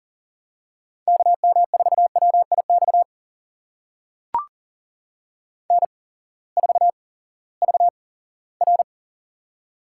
Morse code, heard as a keyed tone, about 40 wpm, spaced as for 20 wpm, 700 Hz. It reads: KM4WIX N4VR
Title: USA Call Signs - 8x Character Spacing - 40wpm